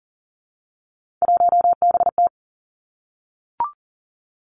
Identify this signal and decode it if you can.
Morse code, heard as a keyed tone, about 40 wpm, 700 Hz. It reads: 1BT
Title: Three Challenging Letter-Number Combinations 40wpm